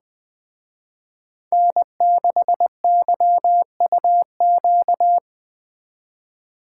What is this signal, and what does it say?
Morse code, heard as a keyed tone, about 24 wpm, 700 Hz. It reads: N6YUQ
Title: Speed Racing - USA 1x3 Call Signs 20wpm